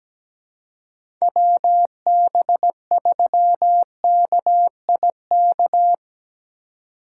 Morse code, heard as a keyed tone, about 17 wpm, 700 Hz. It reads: WB3KIK
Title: USA Call Signs - 6 Characters - 2x Character Spacing - 17wpm